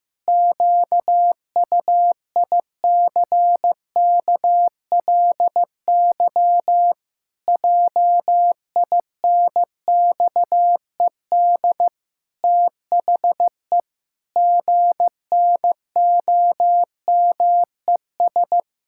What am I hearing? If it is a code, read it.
Morse code, heard as a keyed tone, about 16 wpm, 700 Hz. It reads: QUICKLY JINXED THE GNOMES
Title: Pangram Challenge v2 - 15wpm